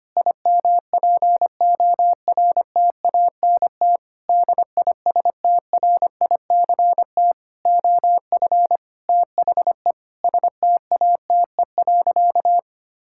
Morse code, heard as a keyed tone, about 25 wpm, 700 Hz. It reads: IMPORTANT DISTRICT OF THE STATE.
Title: Sentences from Top 1600 Words 25wpm